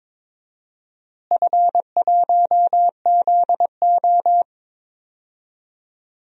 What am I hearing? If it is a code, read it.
Morse code, heard as a keyed tone, about 22 wpm, 700 Hz. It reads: F1ZO
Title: DXCC-100 Call Signs - 4 Characters - Repeated 2x - 22wpm